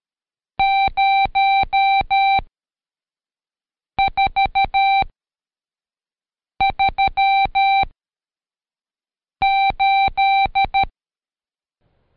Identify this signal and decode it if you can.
Morse code, heard as a keyed tone, about 13 wpm, 780 Hz. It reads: 0 4 3 8
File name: Code Quick / AudioDownld2 / Seg99-Day11-CheckTest 11